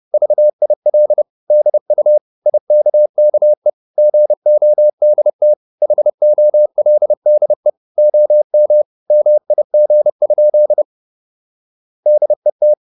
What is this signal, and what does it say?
Morse code, heard as a keyed tone, about 30 wpm, 600 Hz. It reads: VIL DU IKKE GODT HOLDE OM MIG? DET
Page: Ferie 30 wpm